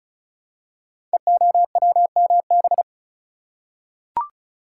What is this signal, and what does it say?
Morse code, heard as a keyed tone, about 34 wpm, 700 Hz. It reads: EOWMB
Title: Code Groups - 5 Letter:Numbers - 35wpm